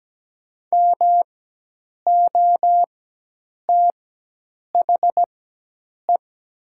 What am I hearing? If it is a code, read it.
Morse code, heard as a keyed tone, about 17 wpm, spaced as for 8 wpm, 690 Hz. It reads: MOTHE
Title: 4x Character Spacing - Top 500 Words - 17wpm